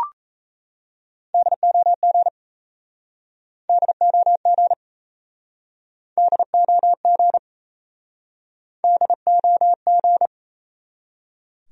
Morse code, heard as a keyed tone, about 39 wpm, 700 Hz. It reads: DOG DOG DOG DOG